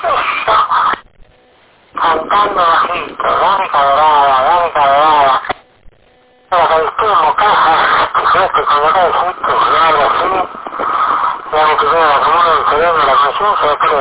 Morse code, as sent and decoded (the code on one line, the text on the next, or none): none